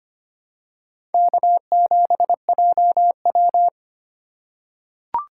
Morse code, heard as a keyed tone, about 25 wpm, 700 Hz. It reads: K7JW
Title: USA Call Signs - 4 Characters - 4x Character Spacing - 25wpm